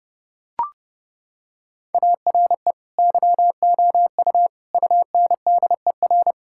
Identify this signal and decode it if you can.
Morse code, heard as a keyed tone, about 32 wpm, 700 Hz. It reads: ARE YOU UNDER
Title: Sentences from Top 1000 Words - Review Part 6 - 30wpm